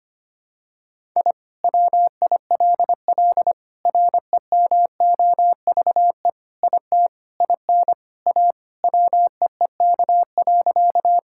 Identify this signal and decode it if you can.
Morse code, heard as a keyed tone, about 25 wpm, 700 Hz. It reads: I WILL REMOVE IT IN A WEEK.
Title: Sentences from Top 1000 Words - Review Part 5 - 25wpm